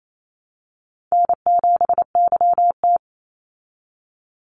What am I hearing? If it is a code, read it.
Morse code, heard as a keyed tone, about 28 wpm, 700 Hz. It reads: N7YT
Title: Easy Call Signs Repeated 2x - 28wpm